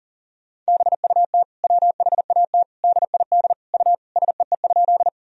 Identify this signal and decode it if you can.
Morse code, heard as a keyed tone, about 37 wpm, 700 Hz. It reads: BUT WHAT DID U SEE?
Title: Sentences from Top 100 Words and Abbreviations 40wpm